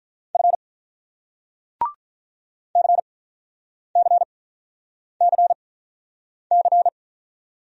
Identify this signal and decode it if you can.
Morse code, heard as a keyed tone, about 48 wpm, spaced as for 9 wpm, 700 Hz. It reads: F CCCC